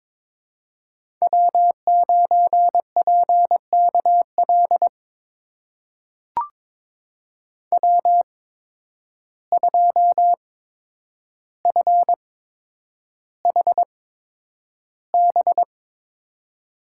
Morse code, heard as a keyed tone, about 21 wpm, spaced as for 9 wpm, 700 Hz. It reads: W9PKL W2FHB
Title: USA Call Signs - 5 Characters - 8x Character Spacing - 22wpm